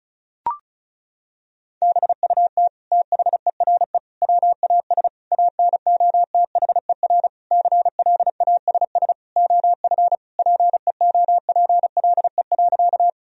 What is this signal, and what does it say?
Morse code, heard as a keyed tone, about 34 wpm, 700 Hz. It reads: BUT THERE WAS ANOTHER CLASS OF PEOPLE.
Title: Sentences from Top 1000 Words - Review Part 2 - 35wpm